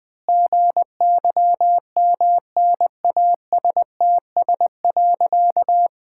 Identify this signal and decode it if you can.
Morse code, heard as a keyed tone, about 20 wpm, 700 Hz. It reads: GYMNASTS.
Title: Pangram Challenge v2 - 20wpm